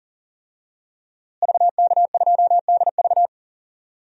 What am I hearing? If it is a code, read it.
Morse code, heard as a keyed tone, about 40 wpm, 700 Hz. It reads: VK2DV
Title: DXCC-100 Call Signs - 5 Characters - Repeated 2x - 40wpm